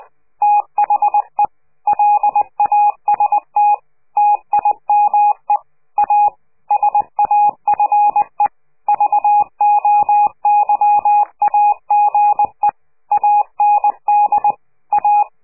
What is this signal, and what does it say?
Morse code, heard as a keyed tone, about 20 wpm, 830 Hz. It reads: THE LAST TIME A SAFE VOYAGE AND A
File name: CQ de PCH laatste bericht farewell